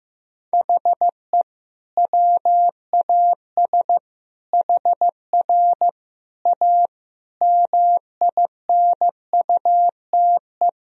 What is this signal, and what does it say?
Morse code, heard as a keyed tone, about 15 wpm, 700 Hz. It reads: HE WAS HR A MINUTE
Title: Sentences from Top 500 Words and Abbreviations 15wpm